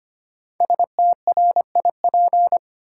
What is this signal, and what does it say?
Morse code, heard as a keyed tone, about 25 wpm, 700 Hz. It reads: STRIP
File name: 5 Letter Words - First 3 Letters Given - 25wpm